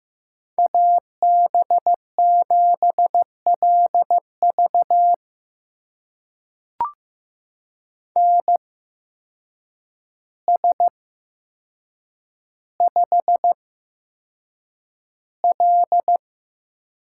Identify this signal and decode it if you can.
Morse code, heard as a keyed tone, about 15 wpm, spaced as for 7 wpm, 700 Hz. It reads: AB7LV NS5L
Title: USA Call Signs - 8x Character Spacing - 15wpm